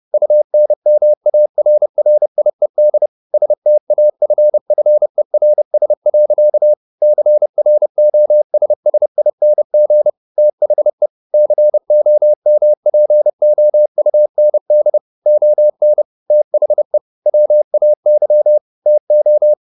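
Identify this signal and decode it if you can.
Morse code, heard as a keyed tone, about 30 wpm, 600 Hz. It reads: UNMARRIED STAFFERS. CROSSING THE COMPOUND ON THE WAY TO
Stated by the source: Morse code at 29 wpm, sending Ken Follett's novel Never in English, from CW med Gnister.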